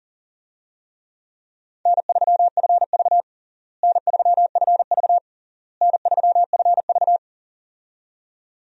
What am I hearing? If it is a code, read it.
Morse code, heard as a keyed tone, about 37 wpm, 700 Hz. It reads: N3FV N3FV N3FV
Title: USA Call Signs - 4 Characters - Repeated 3x - 40wpm